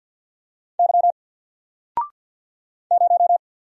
Morse code, heard as a keyed tone, about 50 wpm, 700 Hz. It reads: Y 0
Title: Single Character (Letter:Number:Punctuation:Prosign) 50wpm